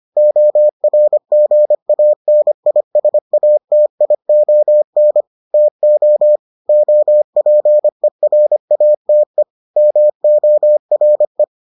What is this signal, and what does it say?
Morse code, heard as a keyed tone, about 25 wpm, 600 Hz. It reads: ORGANISATION TO OPERATE MORE